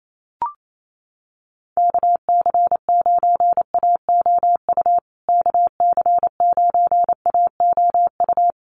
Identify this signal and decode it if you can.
Morse code, heard as a keyed tone, about 28 wpm, 700 Hz. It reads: KC9AOU KC9AOU
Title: USA Call Signs Repeated 3x 28wpm